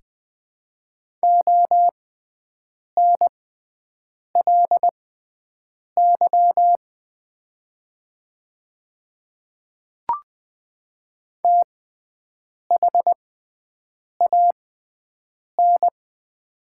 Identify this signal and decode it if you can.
Morse code, heard as a keyed tone, about 20 wpm, spaced as for 7 wpm, 700 Hz. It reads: ONLY THAN
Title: 6x Character Spacing - Top 200 Words - 20wpm